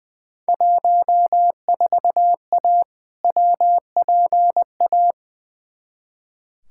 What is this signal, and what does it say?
Morse code, heard as a keyed tone, about 20 wpm, 700 Hz. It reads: 14A WPA
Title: Field Day Exchanges Encore - 20wpm